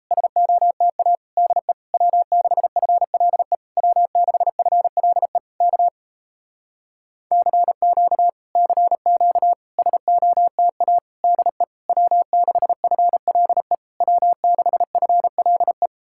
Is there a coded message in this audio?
Yes: Morse code from 0.1 to 15.9 s, heard as a keyed tone, about 37 wpm, 700 Hz. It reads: SOTA DE W6FLE W6FLE K CQ CQ SOTA DE W6FLE W6FLE